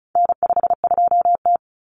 Morse code, heard as a keyed tone, about 34 wpm, 700 Hz. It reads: N52T
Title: Common Cut Numbers - 4-Digit 35wpm